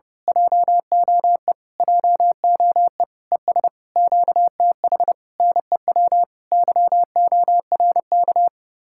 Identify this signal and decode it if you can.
Morse code, heard as a keyed tone, about 28 wpm, 700 Hz. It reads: JOE JOE ES QTH NEW YORK
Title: Speed-Racing Common QSO Phrases - 30wpm